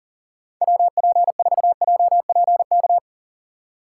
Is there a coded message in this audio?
Yes: Morse code from 0.6 to 3.0 s, heard as a keyed tone, about 52 wpm, 700 Hz. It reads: WP4JPK